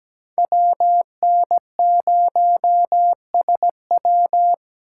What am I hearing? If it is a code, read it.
Morse code, heard as a keyed tone, about 17 wpm, 700 Hz. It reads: WN0SW
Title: USA Call Signs - 5 Characters - 2x Character Spacing - 17wpm